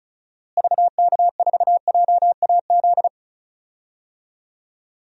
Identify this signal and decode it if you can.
Morse code, heard as a keyed tone, about 36 wpm, 700 Hz. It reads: VK4JAZ
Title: DXCC-100 Call Signs - 6 Characters - Repeated 3x - 35wpm